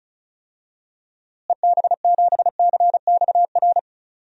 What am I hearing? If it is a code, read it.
Morse code, heard as a keyed tone, about 36 wpm, 700 Hz. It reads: EB7CXR
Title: DXCC-100 Call Signs - 6 Characters - Repeated 3x - 35wpm